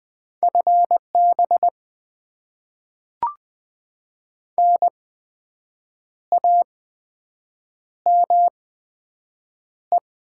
Morse code, heard as a keyed tone, about 20 wpm, spaced as for 7 wpm, 700 Hz. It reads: FB NAME
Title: Top 20 QSO Elements - 8x Character Spacing - 20wpm